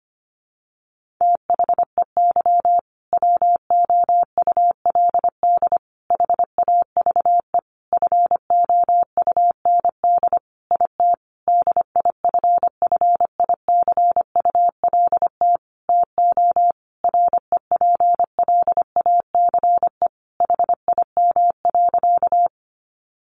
Morse code, heard as a keyed tone, about 25 wpm, 700 Hz. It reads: THEY WOULD HAVE FOUND IT DIFFICULT TO REPLACE HIM.
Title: Sentences from Top 1400 Words 25wpm